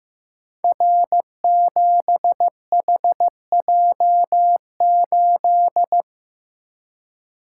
Morse code, heard as a keyed tone, about 15 wpm, 700 Hz. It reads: R7HJ8
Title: Code Groups - 5 Letter:Numbers - 15wpm